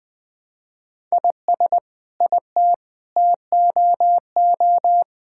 Morse code, heard as a keyed tone, about 20 wpm, 700 Hz. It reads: IS IT TOO
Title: Sentences from Top 1000 Words - Review Part 6 - 20wpm